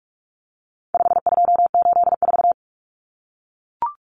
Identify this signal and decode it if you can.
Morse code, heard as a keyed tone, about 44 wpm, 700 Hz. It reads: 5284
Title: Common Cut Numbers - 4-Digit 45wpm